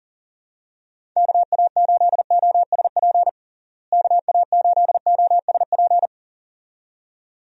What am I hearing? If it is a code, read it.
Morse code, heard as a keyed tone, about 40 wpm, 700 Hz. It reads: KA8OSP KA8OSP
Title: USA Call Signs - 6 Characters - Repeated 2x - 40wpm